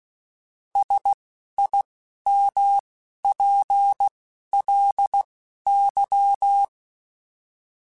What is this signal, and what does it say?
Morse code, heard as a keyed tone, about 16 wpm, spaced as for 11 wpm, 800 Hz. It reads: SIMPLY